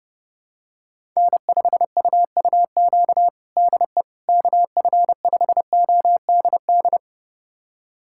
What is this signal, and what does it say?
Morse code, heard as a keyed tone, about 30 wpm, 700 Hz. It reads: N5UUQ DE KF5ODD
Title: USA Callsign DE Callsign 30wpm